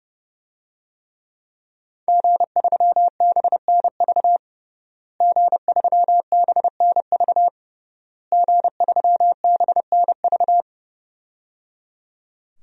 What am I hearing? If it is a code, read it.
Morse code, heard as a keyed tone, about 30 wpm, 700 Hz. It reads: G3BNV G3BNV G3BNV
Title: DXCC-100 Call Signs - 5 Characters - Repeated 3x - 30wpm